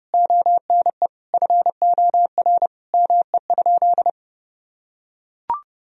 Morse code, heard as a keyed tone, about 30 wpm, 700 Hz. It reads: ONE FOR ME?
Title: Sentences from Top 1000 Words - Review Part 3 - 30wpm